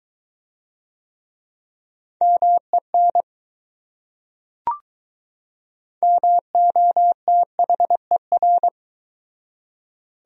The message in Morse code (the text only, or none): MEN MOTHER